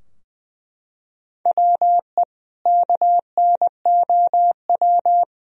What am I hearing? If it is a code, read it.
Morse code, heard as a keyed tone, about 21 wpm, 700 Hz. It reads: WE KNOW
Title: Sentences from Top 1000 Words - Review Part 4 - 20wpm